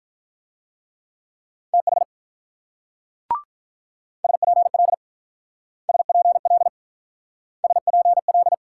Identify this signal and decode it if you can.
Morse code, heard as a keyed tone, about 45 wpm, 700 Hz. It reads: TH SPL SPL SPL